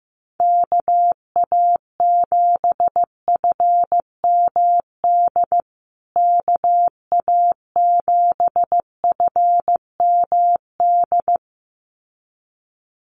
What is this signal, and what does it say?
Morse code, heard as a keyed tone, about 15 wpm, 700 Hz. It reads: KA7FMD KA7FMD
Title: USA Call Signs Repeated 2x 15wpm